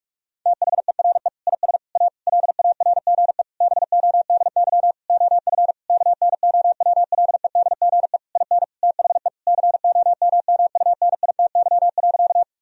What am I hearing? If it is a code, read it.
Morse code, heard as a keyed tone, about 46 wpm, 700 Hz. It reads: THERE IS A LARGE BODY OF KNOWLEDGE IN THE COMMUNITY.
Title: Sentences from Top 1000 Words - Review Part 4 - 45wpm